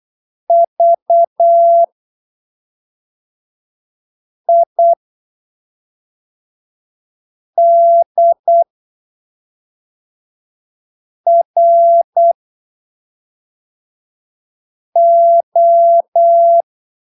Morse code, heard as a keyed tone, about 8 wpm, spaced as for 3 wpm, 680 Hz. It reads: VIDRO
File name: arquivos / texto / texto 11 08ppm